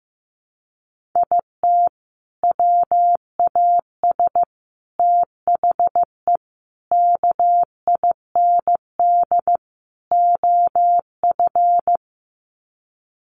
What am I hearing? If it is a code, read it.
Morse code, heard as a keyed tone, about 14 wpm, 700 Hz. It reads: IT WAS THE KIND OF
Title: Sets of 5 Words Encore2 15wpm